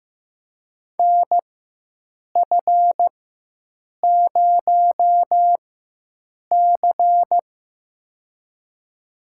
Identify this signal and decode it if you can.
Morse code, heard as a keyed tone, about 15 wpm, spaced as for 7 wpm, 700 Hz. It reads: NF0C
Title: USA Call Signs - 4 Characters - 4x Character Spacing - 15wpm